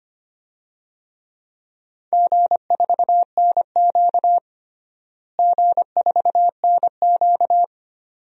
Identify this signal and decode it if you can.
Morse code, heard as a keyed tone, about 25 wpm, 700 Hz. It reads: G4NQ G4NQ
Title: DXCC-100 Call Signs - 4 Characters - Repeated 2x - 25wpm